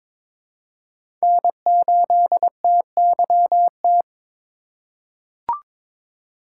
Morse code, heard as a keyed tone, about 22 wpm, 700 Hz. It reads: N8TYT
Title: USA Call Signs - 5 Characters - 2x Character Spacing - 22wpm